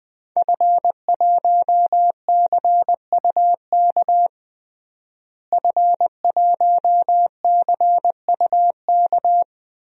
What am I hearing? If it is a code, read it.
Morse code, heard as a keyed tone, about 21 wpm, 700 Hz. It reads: F1CUK F1CUK
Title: DXCC-100 Call Signs - 5 Characters - Repeated 3x - 20wpm